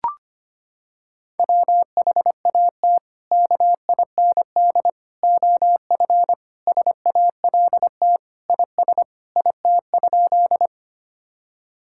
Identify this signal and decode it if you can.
Morse code, heard as a keyed tone, about 25 wpm, 700 Hz. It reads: WHAT KIND OF SALT IS IT?
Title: Sentences from Top 1000 Words - Review Part 5 - 25wpm